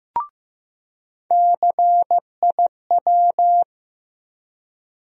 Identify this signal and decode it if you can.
Morse code, heard as a keyed tone, about 15 wpm, 700 Hz. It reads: CIW
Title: USA Call Sign Postfixes - 3 Letters 15wpm